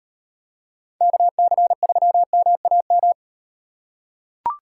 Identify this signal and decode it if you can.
Morse code, heard as a keyed tone, about 37 wpm, 700 Hz. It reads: KC3MAM